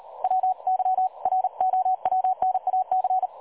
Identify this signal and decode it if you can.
Morse code, heard as a keyed tone, about 38 wpm, 750 Hz. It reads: WYROWNAC
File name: dlazaawansowanych / nagraniazpasma / websdr recording 3565.0kHz